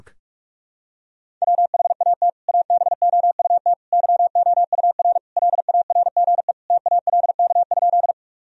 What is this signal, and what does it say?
Morse code, heard as a keyed tone, about 41 wpm, 700 Hz. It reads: WHAT ABOUT YOUR LARGE TALK?
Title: Sentences from Top 1000 Words - Review Part 1 - 45wpm